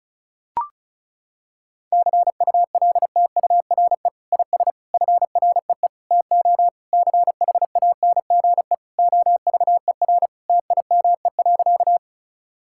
Morse code, heard as a keyed tone, about 36 wpm, 700 Hz. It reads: CULTURE IS FREE TO CHANGE OVER TIME.
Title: Sentences from Top 1000 Words - Review Part 6 - 35wpm